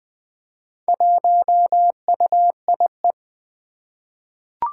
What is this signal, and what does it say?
Morse code, heard as a keyed tone, about 20 wpm, 700 Hz. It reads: 1UIE
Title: Four Challenging Letter-Number Combinations 20wpm